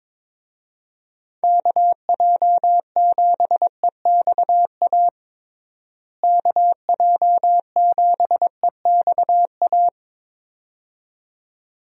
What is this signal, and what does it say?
Morse code, heard as a keyed tone, about 22 wpm, 700 Hz. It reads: KJ7EXA KJ7EXA
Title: USA Call Signs - 6 Characters - Repeated 2x - 22wpm